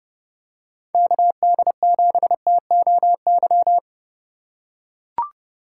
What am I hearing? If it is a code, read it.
Morse code, heard as a keyed tone, about 30 wpm, 700 Hz. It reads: KD7TOY
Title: USA Call Signs - 6 Characters - 2x Character Spacing - 30wpm